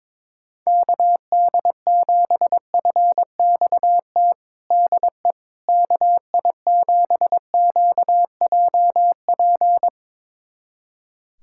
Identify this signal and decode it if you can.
Morse code, heard as a keyed tone, about 22 wpm, 700 Hz. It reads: KD7FXT DE KI7QJP
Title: USA Callsign DE Callsign - 6 Characters - 22wpm